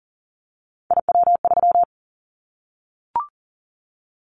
Morse code, heard as a keyed tone, about 40 wpm, 700 Hz. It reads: IW3
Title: Callsign Prefixes for Top 50 DXCC - 40wpm